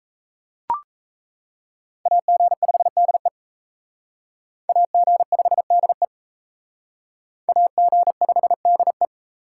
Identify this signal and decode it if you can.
Morse code, heard as a keyed tone, about 34 wpm, 700 Hz. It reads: AG5DE AG5DE AG5DE
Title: Speed Racing - USA 2x2 Call Signs 28wpm